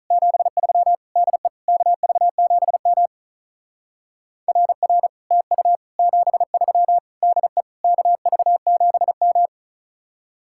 Morse code, heard as a keyed tone, about 42 wpm, 700 Hz. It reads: 73 DE KV7M RR TU 73 DE KV7M
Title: Speed-Racing Common QSO Phrases - 35wpm